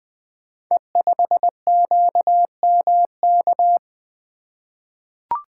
Morse code, heard as a keyed tone, about 20 wpm, 700 Hz. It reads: E5QMK
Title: Code Groups - 5 Letter:Numbers - 20wpm